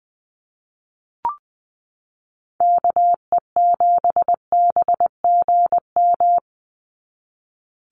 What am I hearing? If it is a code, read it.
Morse code, heard as a keyed tone, about 20 wpm, 700 Hz. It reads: KE7BGM
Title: NA SOTA Activator Callsigns 20wpm